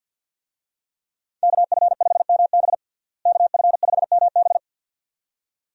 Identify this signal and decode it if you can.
Morse code, heard as a keyed tone, about 50 wpm, 700 Hz. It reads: KF5MB KF5MB
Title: USA Call Signs - 5 Characters - Repeated 2x - 50wpm